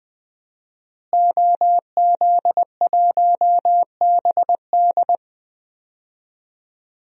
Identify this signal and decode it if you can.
Morse code, heard as a keyed tone, about 20 wpm, 700 Hz. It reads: OZ1BD
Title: DXCC-100 Call Signs - 5 Characters - Repeated 2x - 20wpm